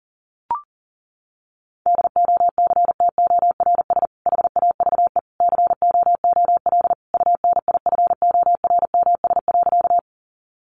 Morse code, heard as a keyed tone, about 40 wpm, 700 Hz. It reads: DOCTORS HAVE COOL UNIFORMS.
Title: Easy Phrases 40wpm